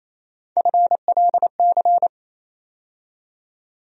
Morse code, heard as a keyed tone, about 28 wpm, 700 Hz. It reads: FLC
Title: USA Call Sign Postfixes - 3 Letters 28wpm